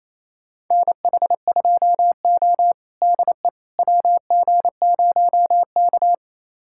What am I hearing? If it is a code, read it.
Morse code, heard as a keyed tone, about 28 wpm, 700 Hz. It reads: NH2O DE WG0K
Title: USA Callsign DE Callsign - 4 Characters - 28wpm